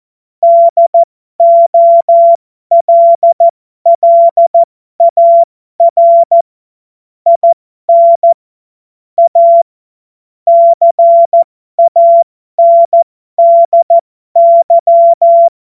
Morse code, heard as a keyed tone, about 14 wpm, spaced as for 12 wpm, 680 Hz. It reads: DOLLAR IN A CANDY